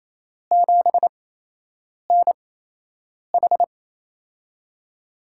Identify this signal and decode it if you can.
Morse code, heard as a keyed tone, about 28 wpm, spaced as for 8 wpm, 700 Hz. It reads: 7NH
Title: USA Call Signs - 8x Character Spacing - 28wpm